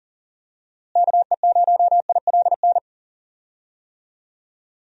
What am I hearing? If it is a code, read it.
Morse code, heard as a keyed tone, about 38 wpm, 700 Hz. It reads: KE0ILN
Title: USA Call Signs - 6 Characters - Repeated 2x - 40wpm